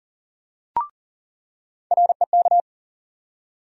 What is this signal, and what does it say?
Morse code, heard as a keyed tone, about 40 wpm, 700 Hz. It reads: REK
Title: USA Call Sign Postfixes - 3 Letters 40wpm